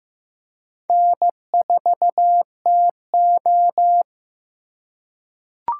Morse code, heard as a keyed tone, about 15 wpm, 700 Hz. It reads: N4TO
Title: USA Call Signs - 4 Characters - 3x Character Spacing - 15wpm